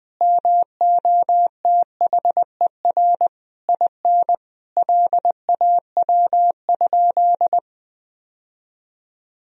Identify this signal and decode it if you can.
Morse code, heard as a keyed tone, about 20 wpm, 700 Hz. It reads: MOTHER IN LAW?
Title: Sentences from Top 1000 Words - Review Part 4 - 20wpm